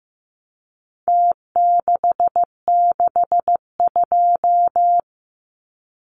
Morse code, heard as a keyed tone, about 15 wpm, 700 Hz. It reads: T662
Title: Numbers - Contest Style - 4-Digit 15wpm